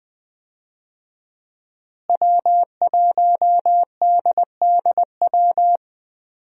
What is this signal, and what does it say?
Morse code, heard as a keyed tone, about 20 wpm, 700 Hz. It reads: W1DDW
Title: USA Call Signs - 5 Characters - Repeated 2x - 20wpm